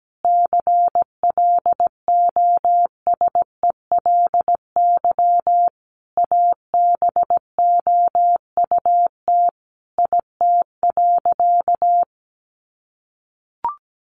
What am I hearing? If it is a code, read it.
Morse code, heard as a keyed tone, about 17 wpm, 700 Hz. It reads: CLOSELY ABOUT IT.
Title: Sentences from Top 1400 Words 17wpm